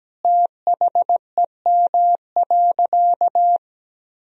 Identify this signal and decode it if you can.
Morse code, heard as a keyed tone, about 17 wpm, 700 Hz. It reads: THEM.
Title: Sentences from Top 200 Words and Abbreviations 17wpm